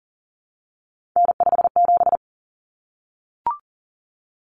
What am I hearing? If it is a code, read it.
Morse code, heard as a keyed tone, about 40 wpm, 700 Hz. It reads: N57
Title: Common Cut Numbers - 3-Digit 40wpm